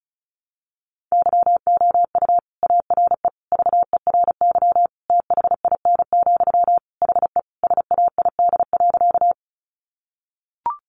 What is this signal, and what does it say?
Morse code, heard as a keyed tone, about 34 wpm, 700 Hz. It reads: YOU ARE VERY THIN, HE SAID.
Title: Sentences from Top 1200 Words 35wpm